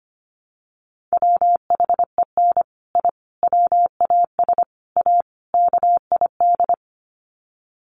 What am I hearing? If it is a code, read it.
Morse code, heard as a keyed tone, about 25 wpm, 700 Hz. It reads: WHEN I WAS A KID
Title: Sets of 5 Words 25wpm